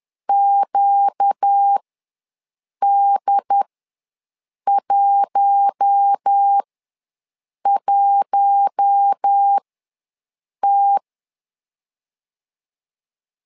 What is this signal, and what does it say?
Morse code, heard as a keyed tone, about 10 wpm, spaced as for 6 wpm, 800 Hz. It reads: QD11T
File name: LEZIONI / 4 Q1 ws 10wpm 2538